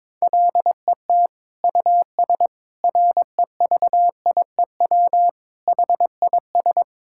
Morse code, heard as a keyed tone, about 21 wpm, 700 Hz. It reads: LET US REVIEW HIS
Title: Sentences from Top 1000 Words - Review Part 6 - 22wpm